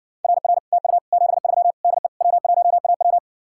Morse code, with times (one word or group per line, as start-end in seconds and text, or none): 0.2-0.6 RR
0.7-1.0 TU
1.1-1.7 73
1.8-2.1 DE
2.2-3.2 W1AW